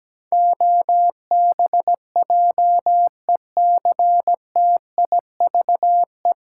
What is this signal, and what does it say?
Morse code, heard as a keyed tone, about 17 wpm, 700 Hz. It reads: OBJECTIVE
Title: Sentences from Top 1200 Words - New Word Drill Alt - 17wpm